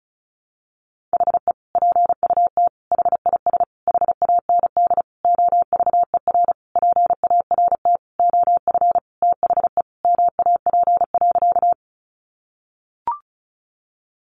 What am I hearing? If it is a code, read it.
Morse code, heard as a keyed tone, about 34 wpm, 700 Hz. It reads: HE PUT HIS HAND OVER PART OF THE MAP.
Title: Sentences from Top 400 Words 35wpm